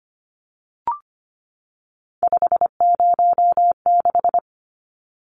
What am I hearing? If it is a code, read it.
Morse code, heard as a keyed tone, about 25 wpm, 700 Hz. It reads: 506
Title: Numbers - 3-Digit 25wpm